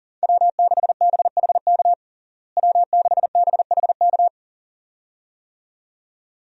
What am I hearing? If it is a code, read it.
Morse code, heard as a keyed tone, about 40 wpm, 700 Hz. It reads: W6BHK W6BHK
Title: USA Call Signs - 5 Characters - Repeated 2x - 40wpm